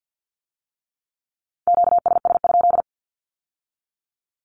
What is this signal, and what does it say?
Morse code, heard as a keyed tone, about 50 wpm, 700 Hz. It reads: QSS?
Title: Q-Codes with Spoken Meaning - 50wpm